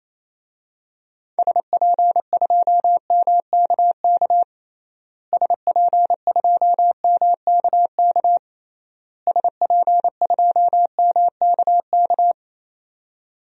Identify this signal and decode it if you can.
Morse code, heard as a keyed tone, about 28 wpm, 700 Hz. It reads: SP2MKK SP2MKK SP2MKK
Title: DXCC-100 Call Signs - 6 Characters - Repeated 3x - 28wpm